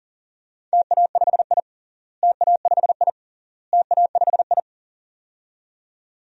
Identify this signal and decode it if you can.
Morse code, heard as a keyed tone, about 40 wpm, 700 Hz. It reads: TA5I TA5I TA5I
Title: DXCC-100 Call Signs - 4 Characters - Repeated 3x - 40wpm